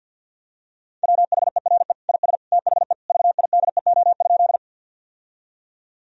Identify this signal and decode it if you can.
Morse code, heard as a keyed tone, about 50 wpm, 700 Hz. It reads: WHERE IS THE VIDEO?
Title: Sentences from Top 1000 Words - Review Part 4 - 50wpm